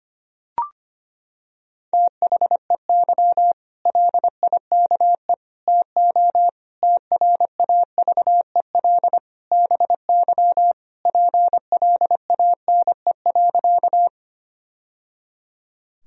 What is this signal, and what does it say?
Morse code, heard as a keyed tone, about 25 wpm, 700 Hz. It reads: THEY LIKE TO TRAVEL BY PLANE.
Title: Sentences from Top 1000 Words - Review Part 3 - 25wpm